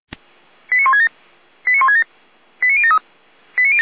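Five-tone selective call (ZVEI), heard as quick runs of five tones, tones of about 80 ms. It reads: 89167 89167 89073
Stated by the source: Five-tone ZVEI selective call received on an SDR.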